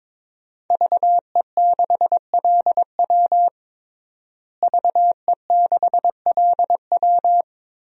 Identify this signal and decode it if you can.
Morse code, heard as a keyed tone, about 22 wpm, 700 Hz. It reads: VE6LW VE6LW
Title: DXCC-100 Call Signs - 5 Characters - Repeated 3x - 22wpm